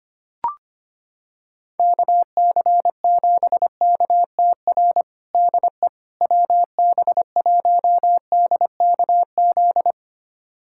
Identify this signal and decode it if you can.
Morse code, heard as a keyed tone, about 25 wpm, 700 Hz. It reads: KC7KTR DE WB1DKZ
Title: USA Callsign DE Callsign 25wpm